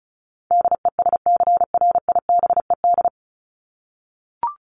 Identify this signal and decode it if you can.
Morse code, heard as a keyed tone, about 34 wpm, 700 Hz. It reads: DESCRIBED
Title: ED Suffix - Top 100 Words with Base Words 35wpm